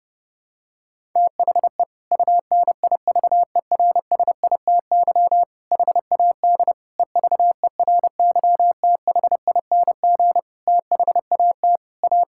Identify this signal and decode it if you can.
Morse code, heard as a keyed tone, about 32 wpm, 700 Hz. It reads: THE UNIVERSITY HAD EVERYTHING THAT A
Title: Sentences from Top 1000 Words - Review Part 5 - 30wpm